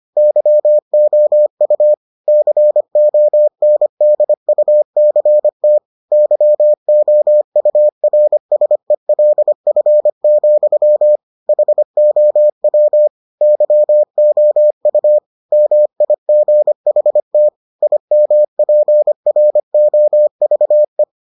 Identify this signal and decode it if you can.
Morse code, heard as a keyed tone, about 25 wpm, 600 Hz. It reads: YOU CONDUCT YOURSELF, HOW YOU MIGHT IMPROVE